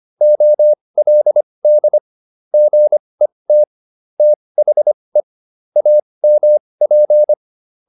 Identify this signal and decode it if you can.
Morse code, heard as a keyed tone, about 25 wpm, spaced as for 20 wpm, 600 Hz. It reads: OLD GET THE AMP